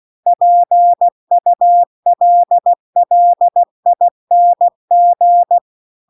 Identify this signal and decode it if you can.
Morse code, heard as a keyed tone, about 16 wpm, 700 Hz. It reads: PULLING
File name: CW-The-Big-Twoheaded-River-2-16WPM